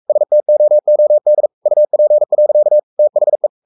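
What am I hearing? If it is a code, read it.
Morse code, heard as a keyed tone, about 44 wpm, 600 Hz. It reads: STOOD UP. THE
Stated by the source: Morse code at 43 wpm sending Ken Follett's novel Never in English, from CW med Gnister.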